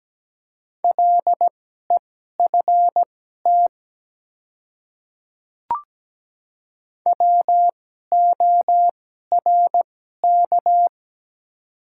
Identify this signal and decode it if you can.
Morse code, heard as a keyed tone, about 17 wpm, spaced as for 12 wpm, 700 Hz. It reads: LEFT WORK
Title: 2x Character Spacing - Top 300 Words - 17wpm